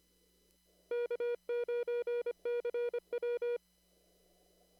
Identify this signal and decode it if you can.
Morse code, heard as a keyed tone, about 25 wpm, 480 Hz. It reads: K9CW
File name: CW / kt5j wpx cw-op-k5pi